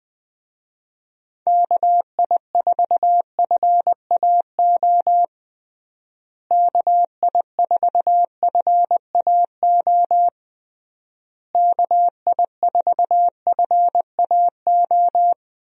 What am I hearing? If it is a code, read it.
Morse code, heard as a keyed tone, about 21 wpm, 700 Hz. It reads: KI4FAO KI4FAO KI4FAO
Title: USA Call Signs - 6 Characters - Repeated 3x - 20wpm